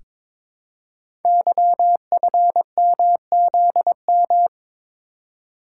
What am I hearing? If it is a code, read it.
Morse code, heard as a keyed tone, about 22 wpm, 700 Hz. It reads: YFMZM
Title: Code Groups - 5 Letters - 22wpm